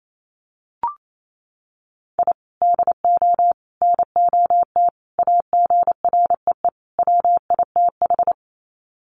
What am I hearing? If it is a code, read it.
Morse code, heard as a keyed tone, about 28 wpm, 700 Hz. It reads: I DO NOT AGREE WITH
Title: Sets of 5 Words Encore 28wpm